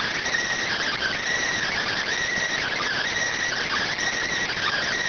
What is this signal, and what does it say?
Emergency Alert System (SAME) tones, heard as harsh buzzing bursts. It is heard from 0.7 to 4.9 s.